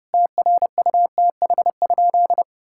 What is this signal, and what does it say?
Morse code, heard as a keyed tone, about 32 wpm, 700 Hz. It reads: TRUTH?
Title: Sentences from Top 1600 Words 30wpm